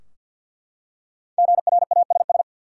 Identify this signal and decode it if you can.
Morse code, heard as a keyed tone, about 52 wpm, 700 Hz. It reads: GRASS
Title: Five Letter Words - 50wpm